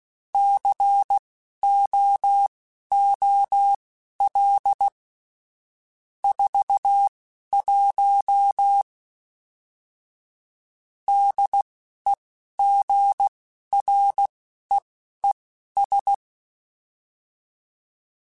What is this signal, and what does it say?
Morse code, heard as a keyed tone, about 16 wpm, spaced as for 12 wpm, 800 Hz. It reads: COOL 41 DEGREES
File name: Weather Sets / Weather Sets-mp3s / Weather 4 14 @10